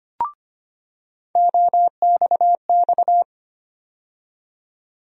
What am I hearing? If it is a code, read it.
Morse code, heard as a keyed tone, about 25 wpm, 700 Hz. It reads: OXX
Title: USA Call Sign Postfixes - 3 Letters 25wpm